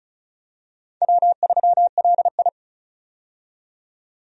Morse code, heard as a keyed tone, about 36 wpm, 700 Hz. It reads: W3LI
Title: USA Call Signs - 4 Characters - 35wpm